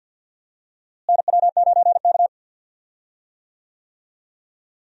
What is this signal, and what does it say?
Morse code, heard as a keyed tone, about 52 wpm, 700 Hz. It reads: NW9K
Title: USA Call Signs - 4 Characters - Repeated 2x - 50wpm